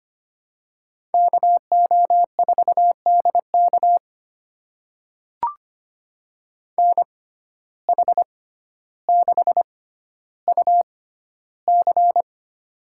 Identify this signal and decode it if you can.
Morse code, heard as a keyed tone, about 25 wpm, spaced as for 13 wpm, 700 Hz. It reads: KO4DK NH6UC
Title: USA Call Signs - 6x Character Spacing - 25wpm